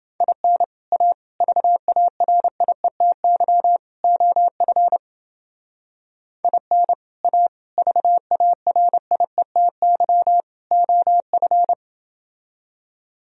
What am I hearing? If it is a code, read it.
Morse code, heard as a keyed tone, about 28 wpm, 700 Hz. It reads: IN A VARIETY OF IN A VARIETY OF